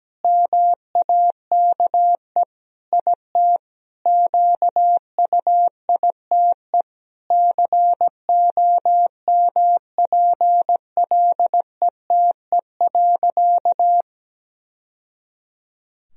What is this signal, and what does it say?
Morse code, heard as a keyed tone, about 17 wpm, 700 Hz. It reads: MAKE IT QUITE COMPLETE.
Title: Sentences from Top 1000 Words - Review Part 2 - 17wpm